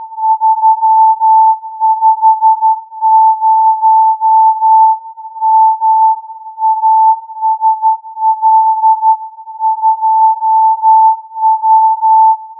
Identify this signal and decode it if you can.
Morse code, heard as a keyed tone, about 12 wpm, 880 Hz. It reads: 350 M ASL 2W